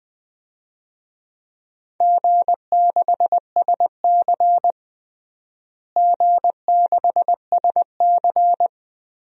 Morse code, heard as a keyed tone, about 20 wpm, 700 Hz. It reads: G6SC G6SC
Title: DXCC-100 Call Signs - 4 Characters - Repeated 3x - 20wpm